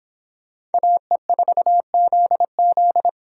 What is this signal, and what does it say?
Morse code, heard as a keyed tone, about 30 wpm, 700 Hz. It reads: AE4ZZ